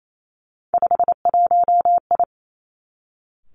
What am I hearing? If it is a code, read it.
Morse code, heard as a keyed tone, about 28 wpm, 700 Hz. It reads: 51I
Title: Three Challenging Letter-Number Combinations 28wpm